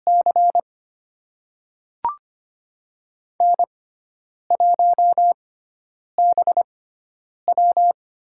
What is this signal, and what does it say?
Morse code, heard as a keyed tone, about 25 wpm, spaced as for 9 wpm, 700 Hz. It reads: C N1BW
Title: USA Call Signs - 4 Characters - 6x Character Spacing - 25wpm